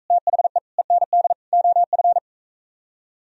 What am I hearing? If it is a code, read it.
Morse code, heard as a keyed tone, about 42 wpm, 700 Hz. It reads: THE END OF